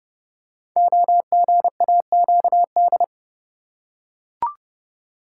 Morse code, heard as a keyed tone, about 30 wpm, 700 Hz. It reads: OGAQD
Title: Code Groups - 5 Letters - 30wpm